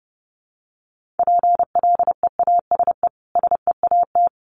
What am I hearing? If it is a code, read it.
Morse code, heard as a keyed tone, about 30 wpm, 700 Hz. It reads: PLEASE SEAT
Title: Sentences from Top 1500 Words 30wpm